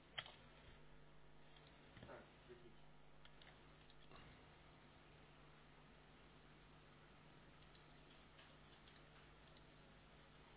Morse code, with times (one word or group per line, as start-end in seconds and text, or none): none